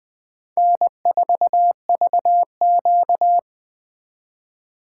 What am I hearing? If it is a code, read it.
Morse code, heard as a keyed tone, about 19 wpm, 700 Hz. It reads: N4VQ